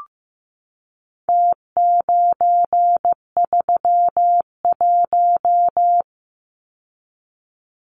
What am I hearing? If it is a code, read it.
Morse code, heard as a keyed tone, about 15 wpm, 700 Hz. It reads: T931
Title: Numbers - Contest Style - 4-Digit 15wpm